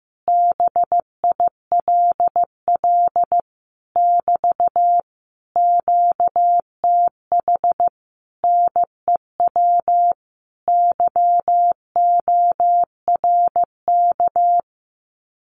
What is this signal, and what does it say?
Morse code, heard as a keyed tone, about 15 wpm, 700 Hz. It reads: BILL <BT> QTH NEW YORK
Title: Common QSO Phrases Encore 15wpm